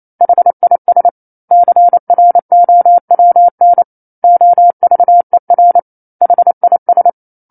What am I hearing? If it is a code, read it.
Morse code, heard as a keyed tone, about 28 wpm, 700 Hz. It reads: HIS CROWN OVER HIS
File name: 09-Chapter-09 28 WPM